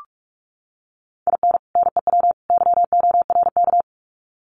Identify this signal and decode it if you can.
Morse code, heard as a keyed tone, about 44 wpm, 700 Hz. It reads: IN NEW YORK
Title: Sets of 3 Words 45wpm